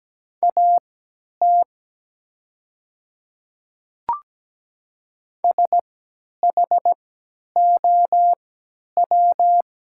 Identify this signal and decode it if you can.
Morse code, heard as a keyed tone, about 17 wpm, spaced as for 10 wpm, 700 Hz. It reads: AT SHOW
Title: 3x Character Spacing - Top 500 Words - 17wpm